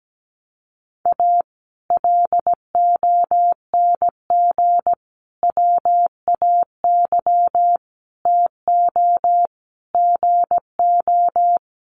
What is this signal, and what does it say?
Morse code, heard as a keyed tone, about 17 wpm, 700 Hz. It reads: A LONG WAY TO GO
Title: Sets of 5 Words 17wpm